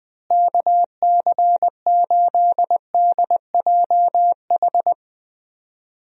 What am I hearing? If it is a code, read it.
Morse code, heard as a keyed tone, about 20 wpm, 700 Hz. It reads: KC8DJH
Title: USA Call Signs - 6 Characters - Repeated 3x - 20wpm